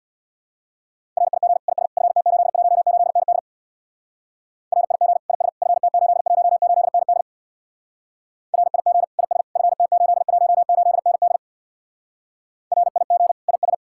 Morse code, heard as a keyed tone, about 62 wpm, 700 Hz. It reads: RIG IS FT818ND RIG IS FT818ND RIG IS FT818ND RIG IS
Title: Speed-Racing Common QSO Phrases - 50wpm